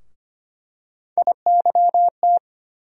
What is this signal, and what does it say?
Morse code, heard as a keyed tone, about 25 wpm, 700 Hz. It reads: IYT
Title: USA Call Sign Postfixes - 3 Letters 25wpm